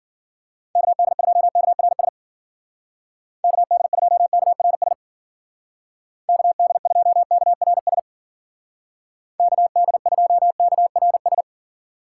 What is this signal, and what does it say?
Morse code, heard as a keyed tone, about 51 wpm, 700 Hz. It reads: KD2KRS KD2KRS KD2KRS KD2KRS